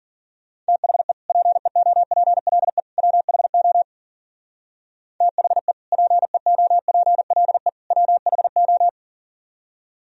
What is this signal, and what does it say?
Morse code, heard as a keyed tone, about 50 wpm, 700 Hz. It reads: THE PEOPLE WHO THE PEOPLE WHO
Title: Speed-Racing - Sets of 3 Words 40wpm